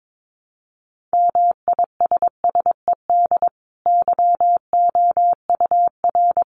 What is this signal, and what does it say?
Morse code, heard as a keyed tone, about 22 wpm, 700 Hz. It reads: MISSED YOUR
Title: Common QSO Phrases 22wpm